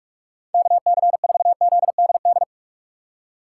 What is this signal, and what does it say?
Morse code, heard as a keyed tone, about 44 wpm, 700 Hz. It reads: KC4ZDD
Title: USA Call Signs - 6 Characters - 2x Character Spacing - 45wpm